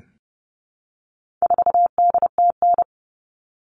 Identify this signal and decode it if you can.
Morse code, heard as a keyed tone, about 30 wpm, 700 Hz. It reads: 4DTN
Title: All Cut Numbers - 4-Digit 30wpm